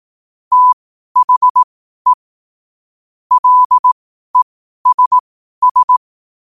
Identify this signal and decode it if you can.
Morse code, heard as a keyed tone, about 18 wpm, spaced as for 12 wpm, 1000 Hz. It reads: THE LESS
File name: qotdfiles / 12wpm